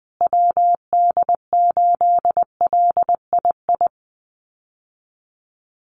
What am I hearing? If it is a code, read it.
Morse code, heard as a keyed tone, about 19 wpm, 700 Hz. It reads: WD8LII
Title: USA Call Signs Repeated 2x 20wpm